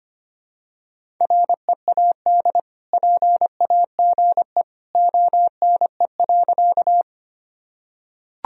Morse code, heard as a keyed tone, about 25 wpm, 700 Hz. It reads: READ PAGE ONE.
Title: Sentences from Top 1000 Words - Review Part 1 - 25wpm